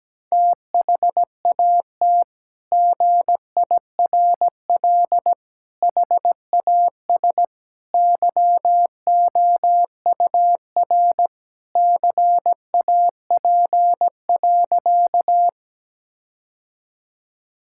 Morse code, heard as a keyed tone, about 17 wpm, 700 Hz. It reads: THAT GIRL HAS YOUR CAP.
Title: Sentences from Top 1600 Words 17wpm